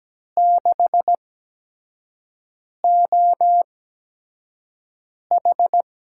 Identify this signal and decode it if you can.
Morse code, heard as a keyed tone, about 17 wpm, spaced as for 5 wpm, 700 Hz. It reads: 6OH
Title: USA Call Signs - 4 Characters - 8x Character Spacing - 17wpm